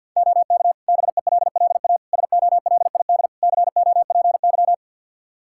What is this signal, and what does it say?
Morse code, heard as a keyed tone, about 52 wpm, 700 Hz. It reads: OK BELLA SOLID COPY
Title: K1USN SST QSO Practice 50wpm